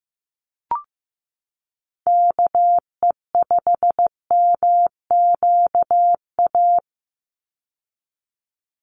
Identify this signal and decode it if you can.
Morse code, heard as a keyed tone, about 15 wpm, 700 Hz. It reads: KE5MQA
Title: USA Call Signs 15wpm